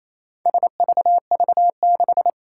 Morse code, heard as a keyed tone, about 28 wpm, 700 Hz. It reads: SVV6
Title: Four Challenging Letter-Number Combinations 28wpm